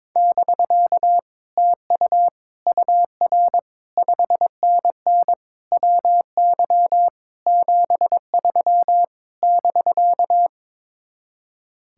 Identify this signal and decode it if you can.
Morse code, heard as a keyed tone, about 22 wpm, 700 Hz. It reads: <BK> TU UR 5NN WY 73 <BK>
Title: Speed-Racing Common QSO Phrases Encore - 22wpm